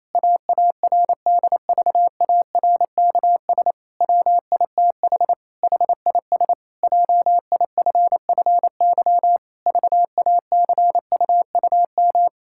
Morse code, heard as a keyed tone, about 27 wpm, 700 Hz. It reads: AARDVARKS WITH HIS JIFFY VACUUM
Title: Pangram Challenge v2 - 28wpm